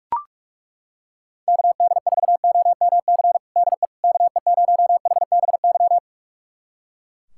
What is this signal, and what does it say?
Morse code, heard as a keyed tone, about 42 wpm, 700 Hz. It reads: KD4OMC DE KE0HBY
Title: USA Callsign DE Callsign - 6 Characters - 45wpm